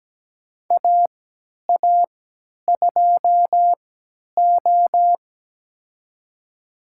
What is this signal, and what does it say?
Morse code, heard as a keyed tone, about 17 wpm, spaced as for 10 wpm, 700 Hz. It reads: AA2O
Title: USA Call Signs - 3x Character Spacing - 17wpm